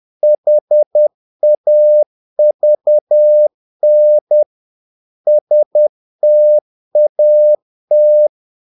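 Morse code, heard as a keyed tone, about 10 wpm, 600 Hz. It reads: HAVN STAT
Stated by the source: Morse practice of short Danish words at 10 wpm, from CW med Gnister.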